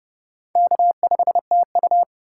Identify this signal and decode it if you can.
Morse code, heard as a keyed tone, about 40 wpm, 700 Hz. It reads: K5TU